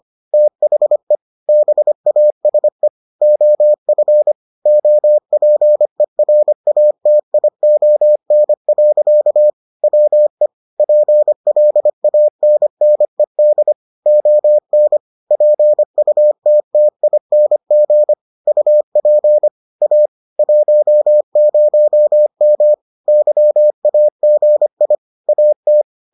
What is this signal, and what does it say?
Morse code, heard as a keyed tone, about 25 wpm, 600 Hz. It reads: THE BASE OF OPERATION. WE PLANNED ON PUTTING UP A 10M YAGI AT